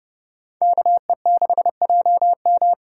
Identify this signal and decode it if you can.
Morse code, heard as a keyed tone, about 37 wpm, 700 Hz. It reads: KE6JM
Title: Speed Racing - USA Call Signs 30wpm